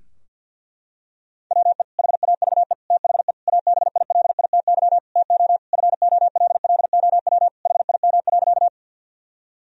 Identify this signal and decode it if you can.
Morse code, heard as a keyed tone, about 52 wpm, 700 Hz. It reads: WE HAVE THE ABILITY TO FOLLOW HIM.
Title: Sentences from Top 1000 Words - Review Part 3 - 50wpm